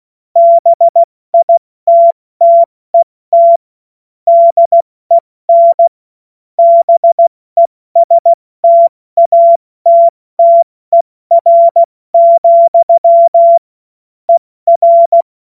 Morse code, heard as a keyed tone, about 16 wpm, 680 Hz. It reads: BITTET DEN BESTATTER, ER